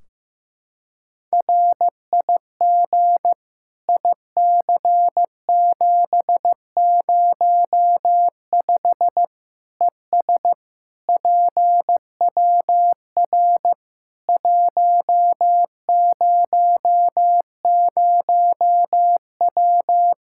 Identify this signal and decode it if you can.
Morse code, heard as a keyed tone, about 16 wpm, 700 Hz. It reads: RIG IC705 ES PWR 100W
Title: Practice QSO - Exchange 2 - Info - 15wpm